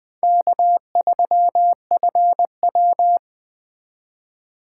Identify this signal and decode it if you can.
Morse code, heard as a keyed tone, about 20 wpm, 700 Hz. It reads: K3FW
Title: Speed Racing - USA Call Signs 17wpm